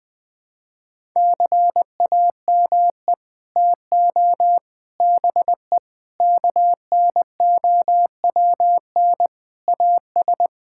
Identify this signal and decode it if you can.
Morse code, heard as a keyed tone, about 20 wpm, 700 Hz. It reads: CAME TO BE KNOWN AS
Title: Sets of 6 Words - 20wpm